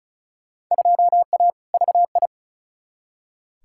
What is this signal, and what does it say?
Morse code, heard as a keyed tone, about 36 wpm, 700 Hz. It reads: 2A VI
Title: Field Day Exchanges - 35wpm